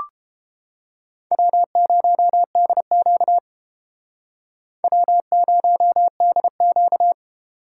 Morse code, heard as a keyed tone, about 32 wpm, 700 Hz. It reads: W0DQ W0DQ